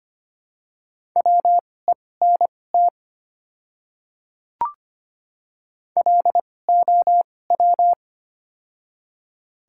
Morse code, heard as a keyed tone, about 25 wpm, spaced as for 18 wpm, 700 Hz. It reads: WENT LOW
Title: 2x Character Spacing - Top 200 Words - 25wpm